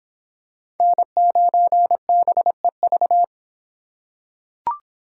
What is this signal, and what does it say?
Morse code, heard as a keyed tone, about 25 wpm, 700 Hz. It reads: N9BEV